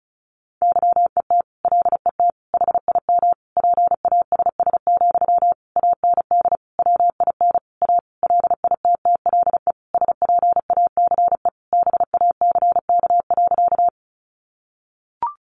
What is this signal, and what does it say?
Morse code, heard as a keyed tone, about 34 wpm, 700 Hz. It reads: YET LET HIM PASS, AND WIN A LITTLE SPACE BACK.
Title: Sentences from Top 1500 Words 35wpm